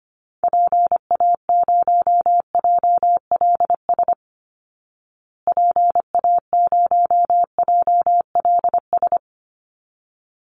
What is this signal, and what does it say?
Morse code, heard as a keyed tone, about 25 wpm, 700 Hz. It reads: PA0JLS PA0JLS
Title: Call Signs Repeated 3x - 25wpm